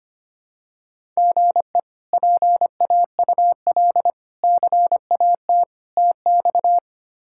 Morse code, heard as a keyed tone, about 25 wpm, 700 Hz. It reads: GE PAUL CAT TX
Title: K1USN SST QSO Practice - 25wpm